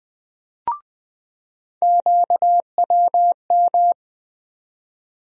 Morse code, heard as a keyed tone, about 20 wpm, 700 Hz. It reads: QWM
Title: USA Call Sign Postfixes - 3 Letters 20wpm